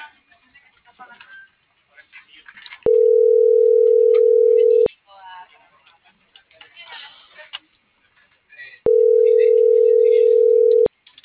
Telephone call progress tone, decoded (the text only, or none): ringback tone